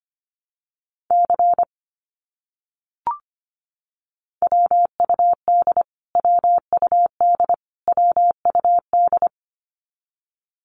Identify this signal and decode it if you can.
Morse code, heard as a keyed tone, about 25 wpm, 700 Hz. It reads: C WUD WUD WUD
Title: Abbreviations Repeated 3x - 25wpm